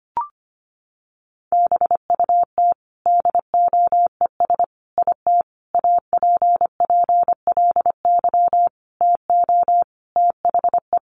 Morse code, heard as a keyed tone, about 25 wpm, 700 Hz. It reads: BUT DOES IT APPLY TO THE
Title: Sentences from Top 1200 Words 25wpm